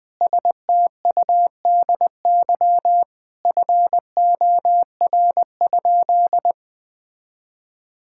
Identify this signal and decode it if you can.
Morse code, heard as a keyed tone, about 20 wpm, 700 Hz. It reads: STUDY FOR?
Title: Sentences from Top 1000 Words - Review Part 6 - 20wpm